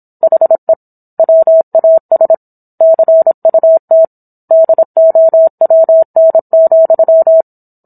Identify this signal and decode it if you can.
Morse code, heard as a keyed tone, about 26 wpm, 650 Hz. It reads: HE WAS CUT DOWN,